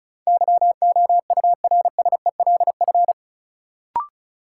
Morse code, heard as a keyed tone, about 36 wpm, 700 Hz. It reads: YOURSELF
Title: Sentences from Top 600 Words - New Word Drill - 35wpm